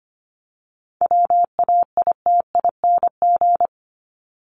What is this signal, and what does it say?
Morse code, heard as a keyed tone, about 25 wpm, 700 Hz. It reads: WAITING
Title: ING Suffix - Top 100 Words with Base Words 25wpm